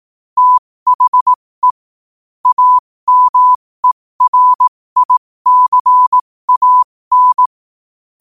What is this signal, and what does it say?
Morse code, heard as a keyed tone, about 18 wpm, spaced as for 15 wpm, 1000 Hz. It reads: THE AMERICAN
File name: qotdfiles / 15wpm